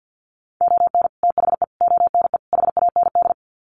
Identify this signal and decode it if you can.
Morse code, heard as a keyed tone, about 50 wpm, 700 Hz. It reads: ON THE ONE HAND
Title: Sets of 4 Words 50wpm